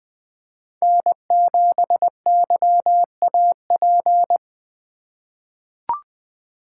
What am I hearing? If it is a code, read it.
Morse code, heard as a keyed tone, about 20 wpm, 700 Hz. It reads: N7YAP
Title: USA Call Signs - 5 Characters - 3x Character Spacing - 20wpm